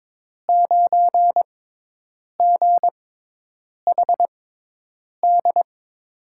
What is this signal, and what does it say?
Morse code, heard as a keyed tone, about 21 wpm, spaced as for 8 wpm, 700 Hz. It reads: 9GHD
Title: USA Call Signs - 6x Character Spacing - 22wpm